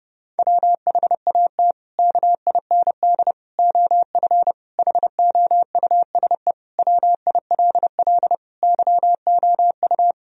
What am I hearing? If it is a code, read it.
Morse code, heard as a keyed tone, about 32 wpm, 700 Hz. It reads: WHAT KIND OF HOUSE WILL YOU
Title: Sentences from Top 1000 Words - Review Part 1 - 30wpm